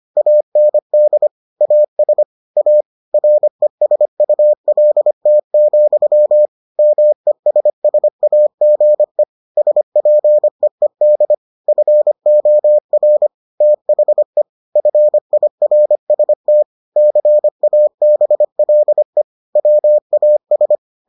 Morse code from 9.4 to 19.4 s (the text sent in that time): SPEED FOR THE FIRST CABLE